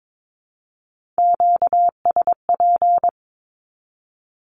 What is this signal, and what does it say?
Morse code, heard as a keyed tone, about 22 wpm, 700 Hz. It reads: QSP
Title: Q-Codes 22wpm